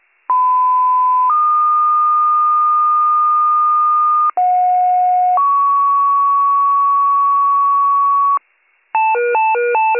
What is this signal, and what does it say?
Two-tone page, heard as a one-second tone, then a three-second tone, about 1010 then 1220 Hz, about 730 then 1120 Hz. It is heard from 0.3 to 8.4 s.